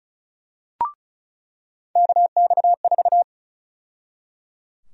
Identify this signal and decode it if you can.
Morse code, heard as a keyed tone, about 34 wpm, 700 Hz. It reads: KX4
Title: USA Call Sign Prefixes - 35wpm